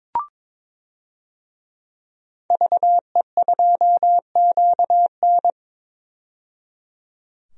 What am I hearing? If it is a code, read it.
Morse code, heard as a keyed tone, about 21 wpm, 700 Hz. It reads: VE2QN
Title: DXCC-100 Call Signs - 5 Characters - 22wpm